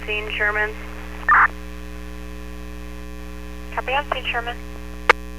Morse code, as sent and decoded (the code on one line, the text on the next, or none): none